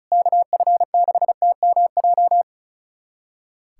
Morse code, heard as a keyed tone, about 36 wpm, 700 Hz. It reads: KF6TMJ
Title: USA Callsign DE Callsign - 6 Characters - 35wpm